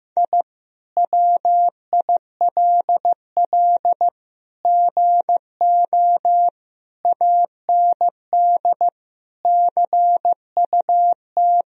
Morse code, heard as a keyed tone, about 14 wpm, 700 Hz. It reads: I WILL GO AND CUT
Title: Sentences from Top 1000 Words - Review Part 1 - 15wpm